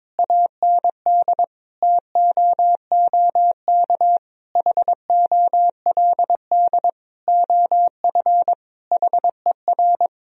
Morse code, heard as a keyed tone, about 22 wpm, 700 Hz. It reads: AND TOOK HOLD OF HER
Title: Sentences from Top 1000 Words - Review Part 2 - 22wpm